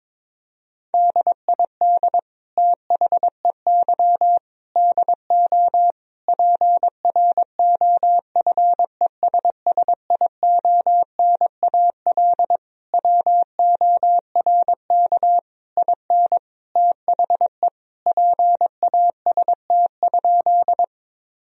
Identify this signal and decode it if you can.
Morse code, heard as a keyed tone, about 22 wpm, 700 Hz. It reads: DID THEY DO PROFESSIONAL WORK IN THE PAST?
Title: Sentences from Top 1000 Words - Review Part 4 - 22wpm